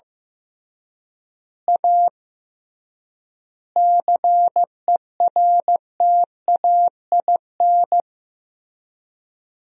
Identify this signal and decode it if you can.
Morse code, heard as a keyed tone, about 15 wpm, 700 Hz. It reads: A CERTAIN
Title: 3x Word-Spacing - Sentences from Top 500 Words - 15wpm